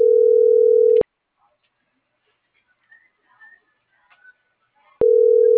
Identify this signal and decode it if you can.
Ringback tone (North America), 440 + 480 Hz.